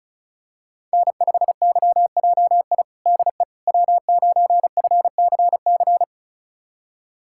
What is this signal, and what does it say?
Morse code, heard as a keyed tone, about 36 wpm, 700 Hz. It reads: N5YJI DE W9FCC
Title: USA Callsign DE Callsign - 5 Characters - 35wpm